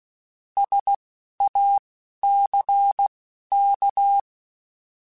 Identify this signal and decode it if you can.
Morse code, heard as a keyed tone, about 16 wpm, spaced as for 11 wpm, 800 Hz. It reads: SACK
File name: Alphabetical / Alphabetical mp3s / Ss @10